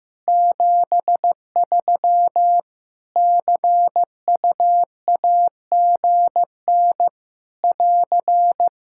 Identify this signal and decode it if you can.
Morse code, heard as a keyed tone, about 15 wpm, 700 Hz. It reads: 73 CUAGN <AR>
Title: Practice QSO - Exchange 4 - Ending - 15wpm